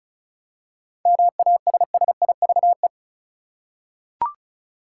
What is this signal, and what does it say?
Morse code, heard as a keyed tone, about 34 wpm, 700 Hz. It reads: MASSIVE
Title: Seven Letter Words - 35wpm